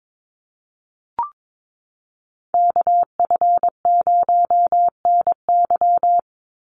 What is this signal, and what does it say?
Morse code, heard as a keyed tone, about 22 wpm, 700 Hz. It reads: KF0NY
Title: USA Call Signs 22wpm